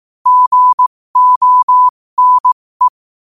Morse code, heard as a keyed tone, about 18 wpm, spaced as for 15 wpm, 1000 Hz. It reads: GONE
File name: qotdfiles / 15wpm